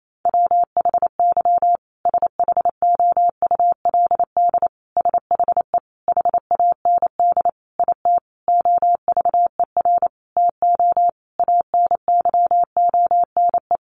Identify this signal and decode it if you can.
Morse code, heard as a keyed tone, about 28 wpm, 700 Hz. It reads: WHY SHOULD SHE HAND IT OVER TO ANYONE
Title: Sentences from Top 900 Words 28wpm